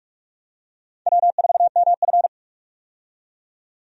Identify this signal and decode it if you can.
Morse code, heard as a keyed tone, about 46 wpm, 700 Hz. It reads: W4MF
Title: USA Call Signs - 4 Characters - 45wpm